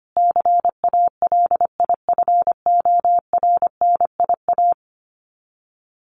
Morse code, heard as a keyed tone, about 25 wpm, 700 Hz. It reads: CALIFORNIA
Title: Sets of 5 Words Encore2 25wpm